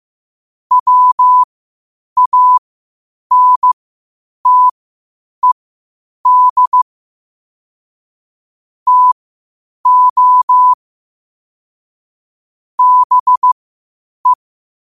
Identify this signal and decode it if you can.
Morse code, heard as a keyed tone, about 15 wpm, spaced as for 8 wpm, 1000 Hz. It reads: WANTED TO BE
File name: qotdfiles / 8wpm